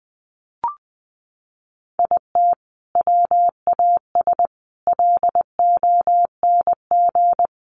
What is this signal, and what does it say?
Morse code, heard as a keyed tone, about 20 wpm, 700 Hz. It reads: IT WAS LONG
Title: Sentences from Top 1400 Words 20wpm